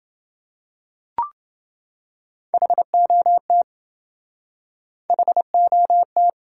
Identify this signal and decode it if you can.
Morse code, heard as a keyed tone, about 27 wpm, 700 Hz. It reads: HOT HOT